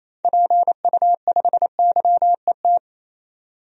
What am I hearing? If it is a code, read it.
Morse code, heard as a keyed tone, about 28 wpm, 700 Hz. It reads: PU5YET
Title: DXCC-100 Call Signs - 6 Characters - 28wpm